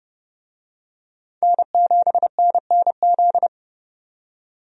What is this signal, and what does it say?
Morse code, heard as a keyed tone, about 30 wpm, 700 Hz. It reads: N7NNZ
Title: USA Call Signs - 5 Characters - 30wpm